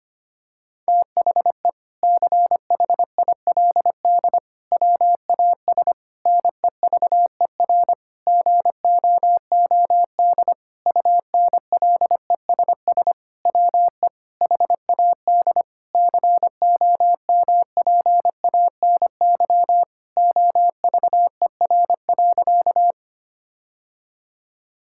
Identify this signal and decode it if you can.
Morse code, heard as a keyed tone, about 25 wpm, 700 Hz. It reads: THE CHILD WAS NEVER GOOD UNLESS WE HAD COMPANY OVER.
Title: Sentences from Top 1000 Words - Review Part 6 - 25wpm